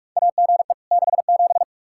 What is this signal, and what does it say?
Morse code, heard as a keyed tone, about 46 wpm, 700 Hz. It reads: AGE 67
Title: Practice QSO - Exchange 3 - More Info - 45wpm